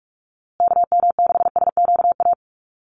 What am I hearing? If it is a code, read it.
Morse code, heard as a keyed tone, about 44 wpm, 700 Hz. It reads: KM6SQA
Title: USA Call Signs Repeated 2x 45wpm